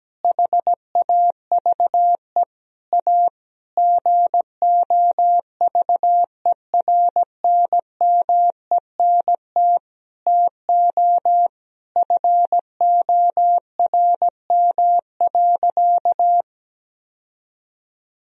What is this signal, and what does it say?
Morse code, heard as a keyed tone, about 17 wpm, 700 Hz. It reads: HAVE A GOVERNMENT TO FORM.
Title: Sentences from Top 1000 Words - Review Part 1 - 17wpm